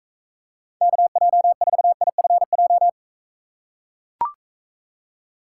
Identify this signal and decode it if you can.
Morse code, heard as a keyed tone, about 43 wpm, 700 Hz. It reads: KJ4IFJ